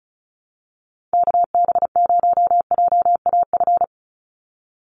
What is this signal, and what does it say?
Morse code, heard as a keyed tone, about 34 wpm, 700 Hz. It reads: KB0JAF
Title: USA Call Signs 35wpm